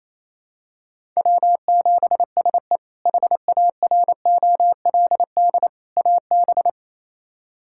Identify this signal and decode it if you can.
Morse code, heard as a keyed tone, about 28 wpm, 700 Hz. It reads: W7SE HAROLD AB
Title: K1USN SST QSO Practice - 28wpm